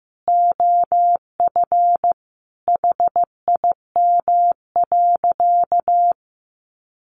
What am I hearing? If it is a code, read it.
Morse code, heard as a keyed tone, about 15 wpm, 700 Hz. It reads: OF HIM.
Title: Sentences from Top 1500 Words 15wpm